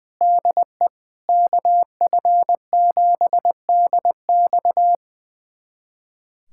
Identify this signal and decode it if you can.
Morse code, heard as a keyed tone, about 20 wpm, 700 Hz. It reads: DE KF7DX
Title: USA Callsign DE Callsign - 5 Characters - 20wpm